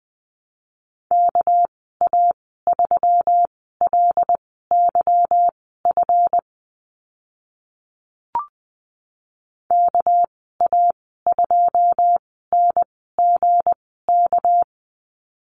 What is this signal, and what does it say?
Morse code, heard as a keyed tone, about 20 wpm, spaced as for 14 wpm, 700 Hz. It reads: KA3LYF KA2NGK
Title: 207 Lesson 33g - Letter j - Call Signs 20wpm